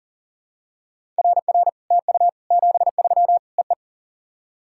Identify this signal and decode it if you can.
Morse code, heard as a keyed tone, about 37 wpm, 700 Hz. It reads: RR TU 73 EE
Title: SOTA QSO Practice 40wpm